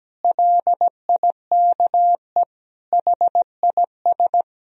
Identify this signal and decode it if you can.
Morse code, heard as a keyed tone, about 17 wpm, 700 Hz. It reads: LIKE HIS
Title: Sentences from Top 1000 Words - Review Part 3 - 17wpm